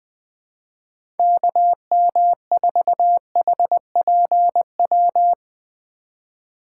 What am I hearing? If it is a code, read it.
Morse code, heard as a keyed tone, about 21 wpm, 700 Hz. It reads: KM4HPW
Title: USA Call Signs - 6 Characters - 4x Character Spacing - 20wpm